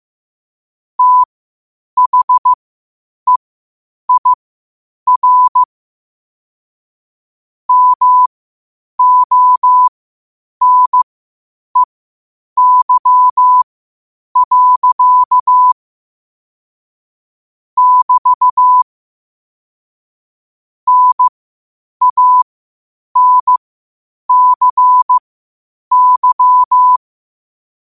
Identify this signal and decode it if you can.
Morse code, heard as a keyed tone, about 15 wpm, spaced as for 8 wpm, 1000 Hz. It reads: THEIR MONEY. <BT> NANCY
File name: qotdfiles / 8wpm